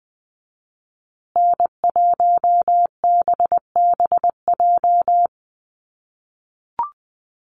Morse code, heard as a keyed tone, about 20 wpm, 700 Hz. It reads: N1BBJ
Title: USA Call Signs Repeated 3x 20wpm